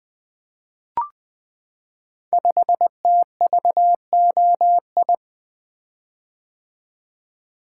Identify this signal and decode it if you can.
Morse code, heard as a keyed tone, about 20 wpm, 700 Hz. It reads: 5TVOI
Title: Code Groups - 5 Letter:Numbers - 20wpm